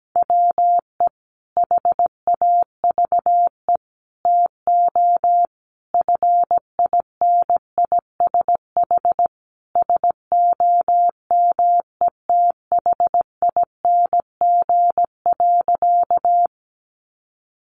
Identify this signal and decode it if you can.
Morse code, heard as a keyed tone, about 17 wpm, 700 Hz. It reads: WE HAVE TO FINISH SOMETHING.
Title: Sentences from Top 1200 Words 17wpm